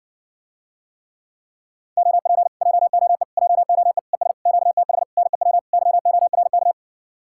Morse code, heard as a keyed tone, about 60 wpm, 700 Hz. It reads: OP JOE JOE ES QTH NEW YORK